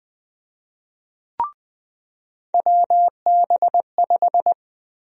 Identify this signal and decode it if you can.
Morse code, heard as a keyed tone, about 20 wpm, 700 Hz. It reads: WB5
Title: USA Call Sign Prefixes - 20wpm